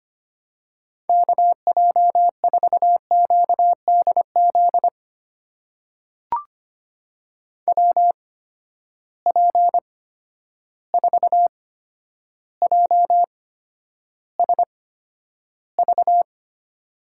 Morse code, heard as a keyed tone, about 25 wpm, spaced as for 11 wpm, 700 Hz. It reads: KJ4QDZ WP4JSV
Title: USA Call Signs - 6 Characters - 8x Character Spacing - 25wpm